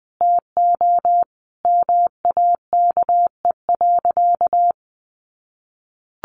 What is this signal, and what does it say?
Morse code, heard as a keyed tone, about 19 wpm, 700 Hz. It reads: TO MAKE.
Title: Sentences from Top 1400 Words 20wpm